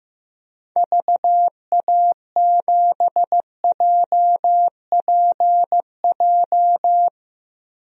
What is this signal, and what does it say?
Morse code, heard as a keyed tone, about 15 wpm, 700 Hz. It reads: VA7JPJ
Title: DXCC-100 Call Signs - 6 Characters - 15wpm